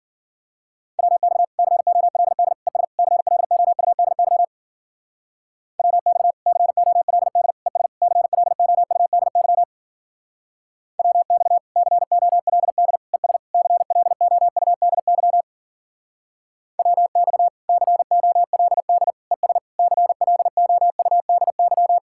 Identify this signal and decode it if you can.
Morse code, heard as a keyed tone, about 51 wpm, 700 Hz. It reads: WX COLD ES CLOUDY WX COLD ES CLOUDY WX COLD ES CLOUDY WX COLD ES CLOUDY